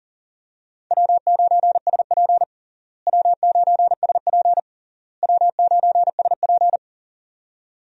Morse code, heard as a keyed tone, about 40 wpm, 700 Hz. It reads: W9SP W9SP W9SP
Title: USA Call Signs - 4 Characters - Repeated 3x - 40wpm